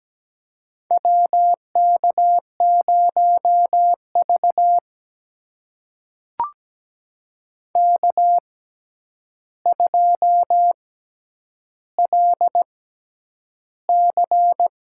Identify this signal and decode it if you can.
Morse code, heard as a keyed tone, about 17 wpm, spaced as for 9 wpm, 700 Hz. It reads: WK0V K2LC
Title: USA Call Signs - 4 Characters - 6x Character Spacing - 17wpm